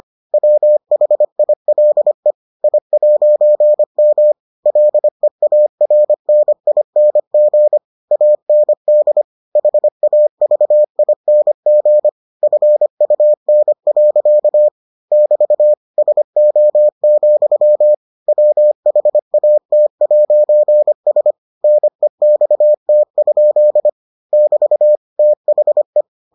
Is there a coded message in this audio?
Yes: Morse code from 0.3 to 26.0 s, heard as a keyed tone, about 25 wpm, 600 Hz. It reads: WHILE I'M LEARNING AND HAVING FUN. <BT> SO, WHAT'S NEXT? <BT> THE